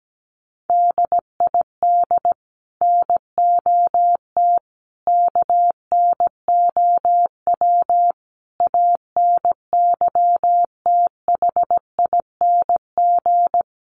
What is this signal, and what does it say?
Morse code, heard as a keyed tone, about 17 wpm, 700 Hz. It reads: DID NOT KNOW ANYTHING
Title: Sets of 5 Words Encore2 17wpm